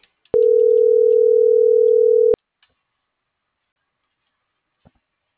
Ringback tone (North America), 440 + 480 Hz.